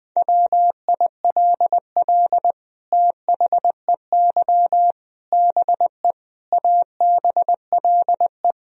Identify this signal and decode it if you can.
Morse code, heard as a keyed tone, about 20 wpm, 700 Hz. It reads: WILL THEY BE ABLE
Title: Sentences from Top 1000 Words - Review Part 3 - 20wpm